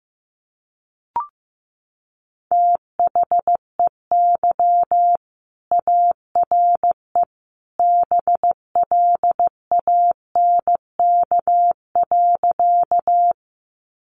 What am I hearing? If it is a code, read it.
Morse code, heard as a keyed tone, about 15 wpm, 700 Hz. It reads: THEY ARE BLANK.
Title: Sentences from Top 1500 Words 15wpm